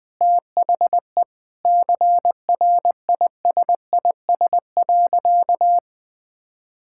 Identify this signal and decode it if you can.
Morse code, heard as a keyed tone, about 20 wpm, 700 Hz. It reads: THE CRISIS.
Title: Sentences with 8-Word Phrases - Spoken Beginning - 20wpm